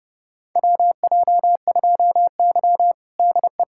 Morse code, heard as a keyed tone, about 30 wpm, 700 Hz. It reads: WJ2Y DE
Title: USA Callsign DE Callsign - 4 Characters - 30wpm